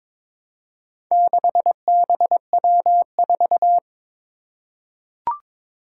Morse code, heard as a keyed tone, about 23 wpm, 700 Hz. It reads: 6BW4
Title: Four Challenging Letter-Number Combinations 22wpm